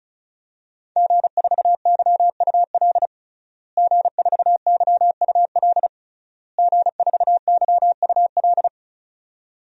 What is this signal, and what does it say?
Morse code, heard as a keyed tone, about 36 wpm, 700 Hz. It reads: G4YUL G4YUL G4YUL
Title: DXCC-100 Call Signs - 5 Characters - Repeated 3x - 35wpm